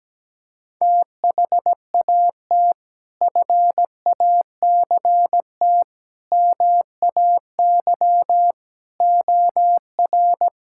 Morse code, heard as a keyed tone, about 17 wpm, 700 Hz. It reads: THAT FACT MAY OR
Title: Sentences from Top 1000 Words - Review Part 6 - 17wpm